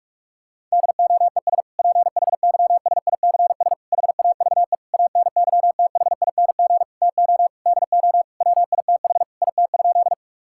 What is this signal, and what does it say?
Morse code, heard as a keyed tone, about 46 wpm, 700 Hz. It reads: DOES PHYSICS HAVE ANYTHING TO DO WITH IT?
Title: Sentences from Top 1000 Words - Review Part 5 - 45wpm